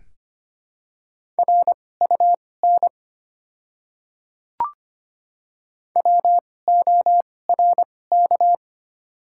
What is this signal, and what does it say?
Morse code, heard as a keyed tone, about 25 wpm, spaced as for 18 wpm, 700 Hz. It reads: RUN WORK
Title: 2x Character Spacing - Top 300 Words - 25wpm